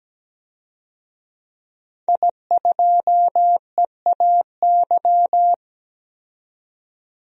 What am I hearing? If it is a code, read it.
Morse code, heard as a keyed tone, about 17 wpm, 700 Hz. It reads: I2EAY
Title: DXCC-100 Call Signs - 5 Characters - 17wpm